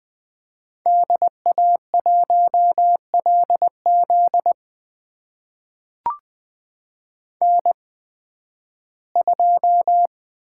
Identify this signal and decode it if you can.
Morse code, heard as a keyed tone, about 20 wpm, spaced as for 13 wpm, 700 Hz. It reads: DA1LZ N2
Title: USA Call Signs - 5 Characters - 8x Character Spacing - 20wpm